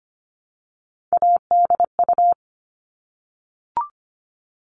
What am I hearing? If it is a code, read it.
Morse code, heard as a keyed tone, about 25 wpm, 700 Hz. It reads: ADU
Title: All Cut Numbers - 3-Digit 25wpm